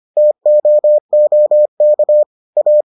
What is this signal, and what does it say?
Morse code, heard as a keyed tone, about 25 wpm, 600 Hz. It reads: TOOK A